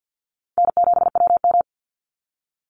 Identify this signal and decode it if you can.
Morse code, heard as a keyed tone, about 50 wpm, 700 Hz. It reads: N7WM
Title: NA SOTA Chaser Callsigns 50wpm